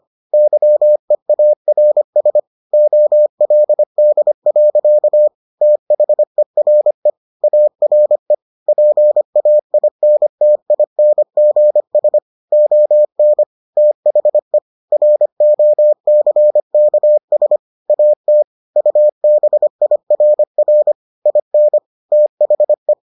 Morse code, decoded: YEARS OLD. THERE ARE PAINTINGS ON THE ROCKS AT UBIRR IN THE

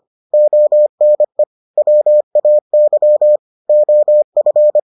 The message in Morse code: ONE WAY OF

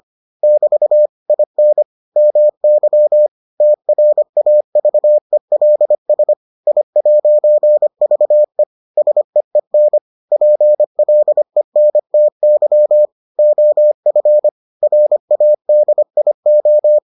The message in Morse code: <BT> IN MY TRAVELS I'VE SEEN PLENTY OF RADIO

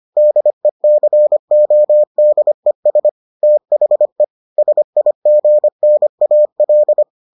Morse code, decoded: DECODES THE SIGNAL

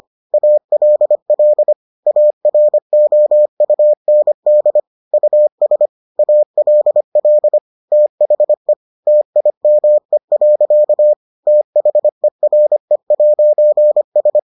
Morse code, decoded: ALL AROUND US ALL THE TIME. THERE'S

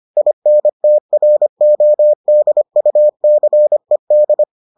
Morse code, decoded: INTRODUCED

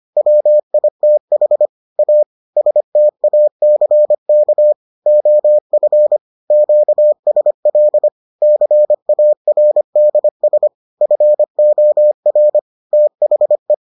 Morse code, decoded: WITH A STACK OF QSL CARDS FOR THE

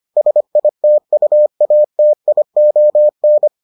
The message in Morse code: SITUATION

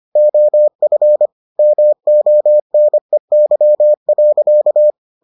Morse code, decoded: OF MONEY.

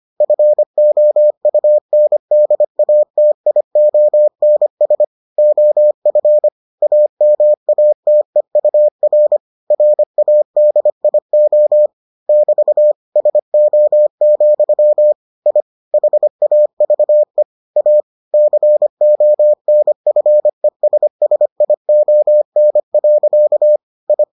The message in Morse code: FOUNDATIONS OF AMATEUR RADIO <BT> SO, I HAVE A CONFESSION. I

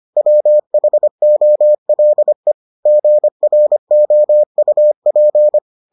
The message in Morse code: WHOLE GROUP

